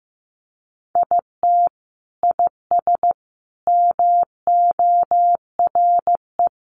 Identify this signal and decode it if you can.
Morse code, heard as a keyed tone, about 15 wpm, 700 Hz. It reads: IT IS MORE
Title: Sentences from Top 1100 Words 15wpm